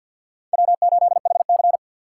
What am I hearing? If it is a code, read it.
Morse code, heard as a keyed tone, about 52 wpm, 700 Hz. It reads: W8HC
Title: USA Call Signs - 6x Character Spacing - 50wpm